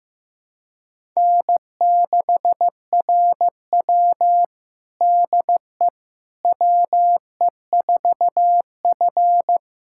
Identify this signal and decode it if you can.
Morse code, heard as a keyed tone, about 15 wpm, 700 Hz. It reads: N6RW DE WE4F
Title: USA Callsign DE Callsign - 4 Characters - 15wpm